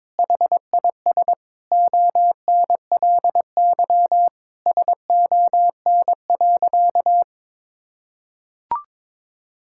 Morse code, heard as a keyed tone, about 22 wpm, 700 Hz. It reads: HIS ONLY SON.
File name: practice / 22wpm / Sentences with 8-Word Phrases - 22wpm